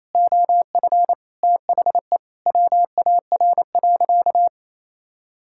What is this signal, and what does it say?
Morse code, heard as a keyed tone, about 28 wpm, 700 Hz. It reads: OF THE WAR.
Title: Sentences from Top 1000 Words - Review Part 6 - 28wpm